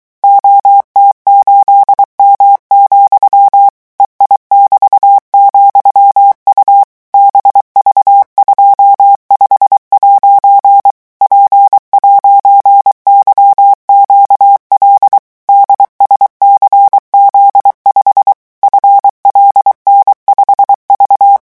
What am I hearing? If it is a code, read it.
Morse code, heard as a keyed tone, about 23 wpm, 800 Hz. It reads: OT8M, EI<BT>,U BV25' P'YQL DSCZ5 FLN5V
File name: morse mp3 / revision13-vitesse 22 mots